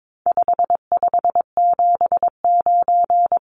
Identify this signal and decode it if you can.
Morse code, heard as a keyed tone, about 22 wpm, 700 Hz. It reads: 5579
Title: Numbers - 4-Digit 22wpm